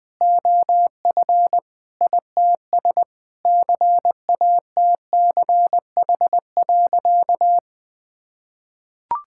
Morse code, heard as a keyed tone, about 21 wpm, 700 Hz. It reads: OF ITS CATCH.
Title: Sentences with 2-Word Phrases - 20wpm